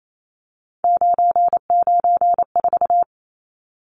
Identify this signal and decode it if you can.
Morse code, heard as a keyed tone, about 28 wpm, 700 Hz. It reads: 994
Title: Numbers - 3-Digit 28wpm